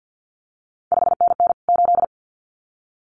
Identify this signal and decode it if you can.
Morse code, heard as a keyed tone, about 50 wpm, 700 Hz. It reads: 5NN 8
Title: CQ WW DX Contest Exchange - 50wpm